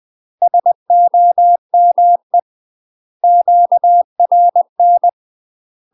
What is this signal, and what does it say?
Morse code, heard as a keyed tone, about 20 wpm, 700 Hz. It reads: SOME QRN
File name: UK MASTER QSOs TRAINING / 20WPM / UK MASTER QSOs 0040